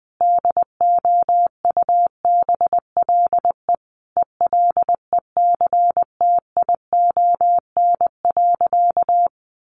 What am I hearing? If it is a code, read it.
Morse code, heard as a keyed tone, about 20 wpm, 700 Hz. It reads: DOUBLE ELECTION.
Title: Sentences from Top 1500 Words 20wpm